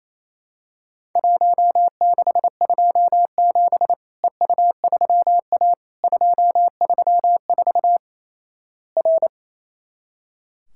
Morse code, heard as a keyed tone, about 27 wpm, 700 Hz. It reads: 1627 EU3A 234 R
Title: Worked All Europe DX Contest - QTC Exchange - 28wpm